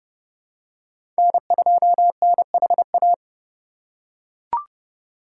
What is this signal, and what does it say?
Morse code, heard as a keyed tone, about 30 wpm, 700 Hz. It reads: N2NHA
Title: USA Call Signs - 5 Characters - 4x Character Spacing - 30wpm